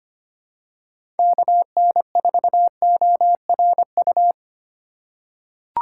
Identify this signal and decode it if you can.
Morse code, heard as a keyed tone, about 25 wpm, 700 Hz. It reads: KN4ORU
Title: USA Call Signs - 6 Characters - 4x Character Spacing - 25wpm